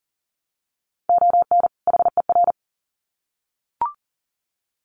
Morse code, heard as a keyed tone, about 40 wpm, 700 Hz. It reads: ON HER
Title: Sets of 2 Words 40wpm